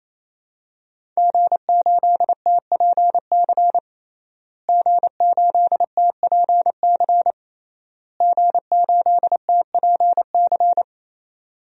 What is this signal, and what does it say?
Morse code, heard as a keyed tone, about 28 wpm, 700 Hz. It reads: G8TPC G8TPC G8TPC
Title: DXCC-100 Call Signs - 5 Characters - Repeated 3x - 28wpm